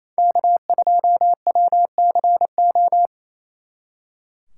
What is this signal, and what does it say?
Morse code, heard as a keyed tone, about 36 wpm, 700 Hz. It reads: K2WCO